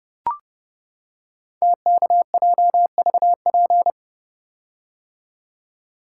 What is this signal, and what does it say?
Morse code, heard as a keyed tone, about 30 wpm, 700 Hz. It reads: TKJVP
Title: Code Groups - 5 Letters - 30wpm